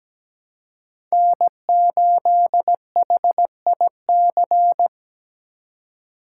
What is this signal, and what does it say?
Morse code, heard as a keyed tone, about 17 wpm, 700 Hz. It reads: N8HIC
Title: USA Call Signs - 5 Characters - 4x Character Spacing - 17wpm